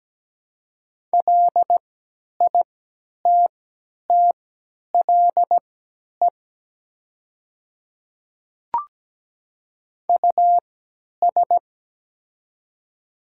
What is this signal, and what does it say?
Morse code, heard as a keyed tone, about 17 wpm, spaced as for 10 wpm, 700 Hz. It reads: LITTLE US
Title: 3x Character Spacing - Top 200 Words - 17wpm